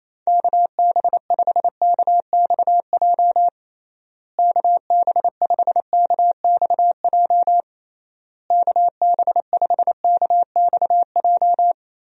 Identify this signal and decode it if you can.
Morse code, heard as a keyed tone, about 28 wpm, 700 Hz. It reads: KB5KXJ KB5KXJ KB5KXJ
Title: USA Call Signs - 6 Characters - Repeated 3x - 28wpm